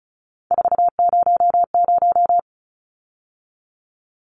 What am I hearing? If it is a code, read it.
Morse code, heard as a keyed tone, about 34 wpm, 700 Hz. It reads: 400
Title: Numbers - 3-Digit 35wpm